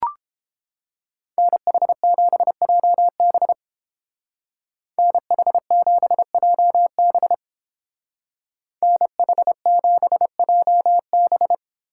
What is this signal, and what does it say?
Morse code, heard as a keyed tone, about 28 wpm, 700 Hz. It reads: NH7JB NH7JB NH7JB